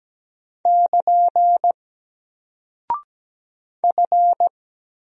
Morse code, heard as a keyed tone, about 17 wpm, 700 Hz. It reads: <KN> F
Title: Single Character (Letter:Number:Punctuation:Prosign) 17wpm